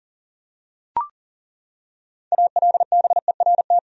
Morse code, heard as a keyed tone, about 40 wpm, 700 Hz. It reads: ALBERT
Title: USA Common Names - 40wpm